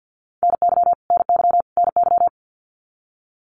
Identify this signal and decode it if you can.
Morse code, heard as a keyed tone, about 50 wpm, 700 Hz. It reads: NY NY NY
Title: US States Repeated 3x 50wpm